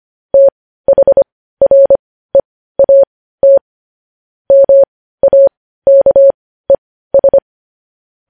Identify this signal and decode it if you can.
Morse code, heard as a keyed tone, about 25 wpm, spaced as for 15 wpm, 550 Hz. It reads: THREAT MAKES
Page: news headlines in morse code at 15 wpm